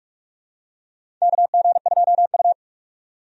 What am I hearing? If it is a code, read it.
Morse code, heard as a keyed tone, about 43 wpm, 700 Hz. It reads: KG2U
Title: USA Call Signs - 4 Characters - 45wpm